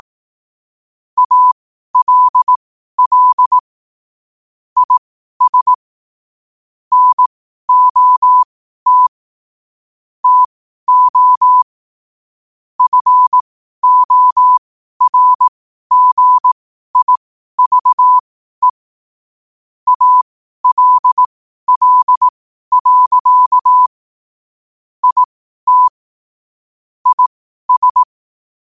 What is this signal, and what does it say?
Morse code, heard as a keyed tone, about 18 wpm, spaced as for 12 wpm, 1000 Hz. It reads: ALL IS NOT TO FORGIVE ALL. IT IS
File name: qotdfiles / 12wpm